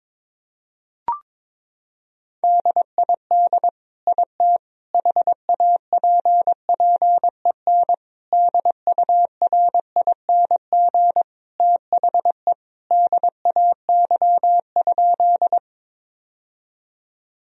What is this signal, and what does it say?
Morse code, heard as a keyed tone, about 22 wpm, 700 Hz. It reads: DID IT HAPPEN DURING THE DAY?
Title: Sentences from Top 1000 Words - Review Part 2 - 22wpm